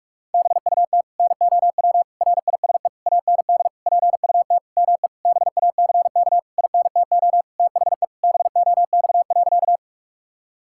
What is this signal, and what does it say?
Morse code, heard as a keyed tone, about 42 wpm, 700 Hz. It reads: BUT NOW RISE AND PUT ME BACK INTO THE BOX.
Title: Sentences from Top 1000 Words - Review Part 6 - 45wpm